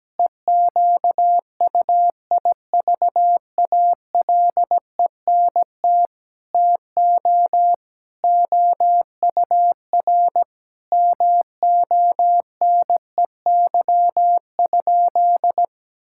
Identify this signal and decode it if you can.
Morse code, heard as a keyed tone, about 17 wpm, 700 Hz. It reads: EQUIVALENT TO OUR MONEY?
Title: Sentences from Top 1600 Words 17wpm